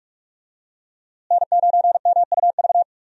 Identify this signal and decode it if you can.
Morse code, heard as a keyed tone, about 46 wpm, 700 Hz. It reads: N9MUV
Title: USA Call Signs - 5 Characters - Repeated 2x - 45wpm